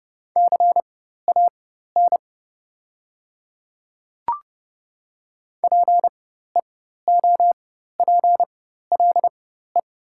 Morse code, heard as a keyed tone, about 30 wpm, spaced as for 14 wpm, 700 Hz. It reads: CAN PEOPLE
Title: 4x Character Spacing - Top 200 Words - 30wpm